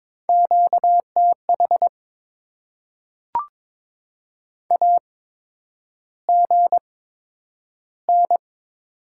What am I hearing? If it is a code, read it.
Morse code, heard as a keyed tone, about 22 wpm, spaced as for 9 wpm, 700 Hz. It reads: QTH AGN
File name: Top 20 QSO Elements - 8x Character Spacing - 22wpm